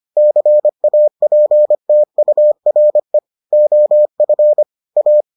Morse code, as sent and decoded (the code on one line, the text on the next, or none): -.-. .- .--. - ..- .-. / --- ..-. / .-
CAPTURE OF A